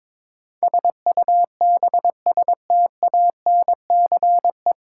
Morse code, heard as a keyed tone, about 22 wpm, 700 Hz. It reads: SUBSTANCE
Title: Sentences from Top 1500 Words - New Word Drill Alt - 22wpm